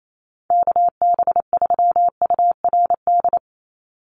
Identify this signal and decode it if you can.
Morse code, heard as a keyed tone, about 28 wpm, 700 Hz. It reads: KB3URD
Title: USA Call Signs 28wpm